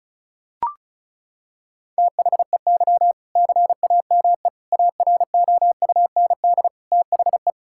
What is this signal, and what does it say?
Morse code, heard as a keyed tone, about 36 wpm, 700 Hz. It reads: THEY CAME AROUND THE
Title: Sentences from Top 1600 Words 35wpm